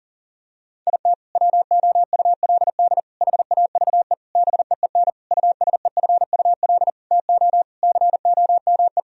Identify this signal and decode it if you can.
Morse code, heard as a keyed tone, about 40 wpm, 700 Hz. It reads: IT WOULD HAVE BEEN USEFUL TO COME
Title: Sentences from Top 1000 Words - Review Part 4 - 40wpm